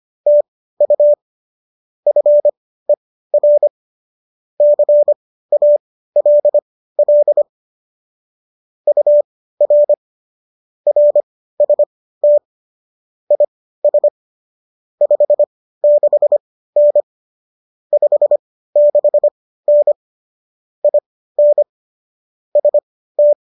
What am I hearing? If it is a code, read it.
Morse code, heard as a keyed tone, about 25 wpm, spaced as for 15 wpm, 590 Hz. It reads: TU FER CALL UR RST IS 56N 56N IN ST